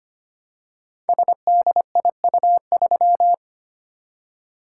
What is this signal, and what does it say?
Morse code, heard as a keyed tone, about 25 wpm, 700 Hz. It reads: SDIU3
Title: Code Groups - 5 Letter:Numbers - 25wpm